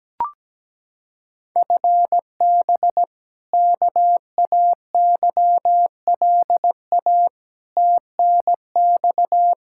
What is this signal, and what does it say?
Morse code, heard as a keyed tone, about 17 wpm, 700 Hz. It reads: FB KAYLA TNX
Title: K1USN SST QSO Practice 17wpm